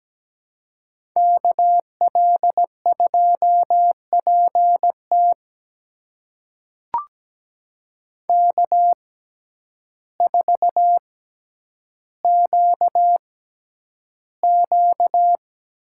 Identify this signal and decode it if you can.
Morse code, heard as a keyed tone, about 17 wpm, spaced as for 9 wpm, 700 Hz. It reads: KL2PT K4QQ
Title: USA Call Signs - 5 Characters - 6x Character Spacing - 17wpm